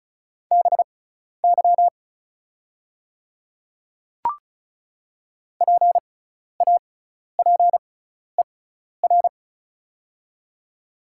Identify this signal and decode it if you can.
Morse code, heard as a keyed tone, about 36 wpm, spaced as for 12 wpm, 690 Hz. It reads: BY PAPER
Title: 6x Character Spacing - Top 300 Words - 35wpm